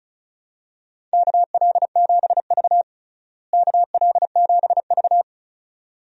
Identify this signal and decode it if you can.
Morse code, heard as a keyed tone, about 34 wpm, 700 Hz. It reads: KL7V KL7V
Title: DXCC-100 Call Signs - 4 Characters - Repeated 2x - 35wpm